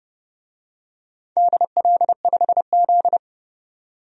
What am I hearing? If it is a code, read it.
Morse code, heard as a keyed tone, about 31 wpm, 700 Hz. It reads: DL5Z
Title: DXCC-100 Call Signs - 4 Characters - 30wpm